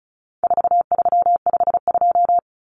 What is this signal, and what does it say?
Morse code, heard as a keyed tone, about 34 wpm, 700 Hz. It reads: 4352
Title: Common Cut Numbers - 4-Digit 35wpm